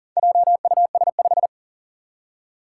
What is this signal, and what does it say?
Morse code, heard as a keyed tone, about 40 wpm, 700 Hz. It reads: JUS5
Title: Four Challenging Letter-Number Combinations 40wpm